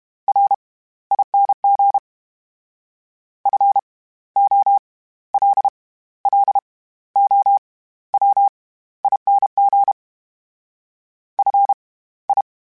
Morse code, heard as a keyed tone, about 32 wpm, spaced as for 16 wpm, 800 Hz. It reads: RING FOLLOWING FI
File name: Special INGs / Special INGs-mp3s / SlowToFaster ing @30